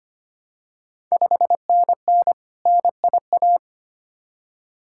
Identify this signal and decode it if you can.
Morse code, heard as a keyed tone, about 25 wpm, 700 Hz. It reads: 5NN NIA
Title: Ontario QSO Party Exchange 25wpm